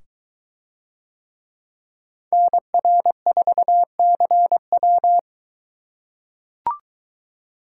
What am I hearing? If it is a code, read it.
Morse code, heard as a keyed tone, about 22 wpm, 700 Hz. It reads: NR4CW